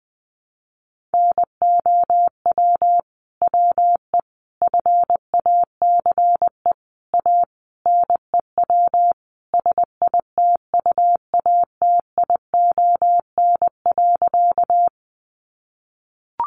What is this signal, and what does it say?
Morse code, heard as a keyed tone, about 19 wpm, 700 Hz. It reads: NOW WE FACE A NEW SITUATION.
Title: Sentences from Top 1200 Words 20wpm